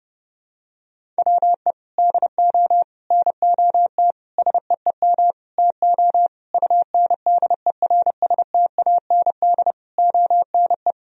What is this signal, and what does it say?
Morse code, heard as a keyed tone, about 30 wpm, 700 Hz. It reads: WE DO NOT SEEM TO UNDERSTAND ONE
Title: Sentences from Top 1000 Words - Review Part 1 - 30wpm